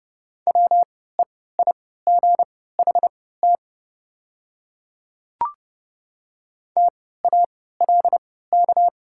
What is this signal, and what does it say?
Morse code, heard as a keyed tone, about 30 wpm, spaced as for 17 wpm, 690 Hz. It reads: WEIGHT TALK
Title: 3x Character Spacing - Top 500 Words - 30wpm